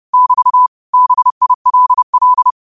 Morse code, heard as a keyed tone, about 30 wpm, 1000 Hz. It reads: <BT> BILL
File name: qotdfiles / 30wpm